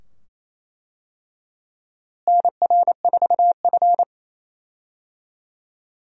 Morse code, heard as a keyed tone, about 28 wpm, 700 Hz. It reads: NR4F
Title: USA Call Signs - 4 Characters - Repeated 2x - 28wpm